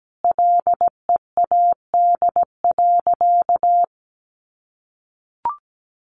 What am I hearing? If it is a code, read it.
Morse code, heard as a keyed tone, about 17 wpm, 700 Hz. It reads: LEAD.
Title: Sentences from Top 1100 Words 17wpm